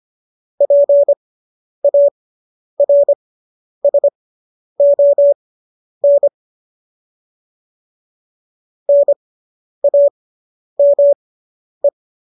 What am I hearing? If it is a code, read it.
Morse code, heard as a keyed tone, about 25 wpm, spaced as for 10 wpm, 570 Hz. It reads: PARSON NAME